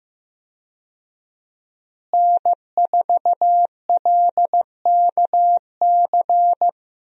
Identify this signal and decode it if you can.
Morse code, heard as a keyed tone, about 15 wpm, 700 Hz. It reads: N4LKC
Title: USA Call Signs - 5 Characters - Repeated 2x - 15wpm